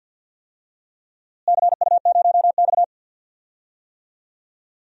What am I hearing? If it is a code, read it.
Morse code, heard as a keyed tone, about 47 wpm, 700 Hz. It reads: CU0X
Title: DXCC-100 Call Signs - 4 Characters - Repeated 2x - 50wpm